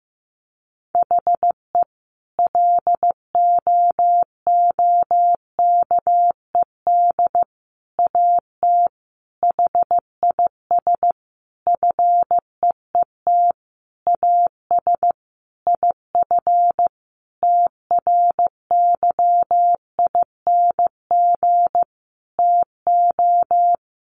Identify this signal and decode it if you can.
Morse code, heard as a keyed tone, about 15 wpm, 700 Hz. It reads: HE LOOKED AT HIS FEET AS IF TRYING TO
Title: A Sound Of Thunder 15wpm